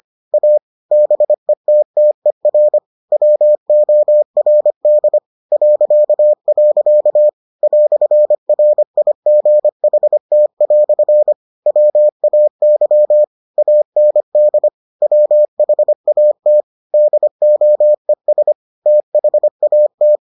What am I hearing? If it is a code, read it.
Morse code, heard as a keyed tone, about 25 wpm, 600 Hz. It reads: A BETTER WORD .. "RIGHT" WAY AND WHAT DOES THAT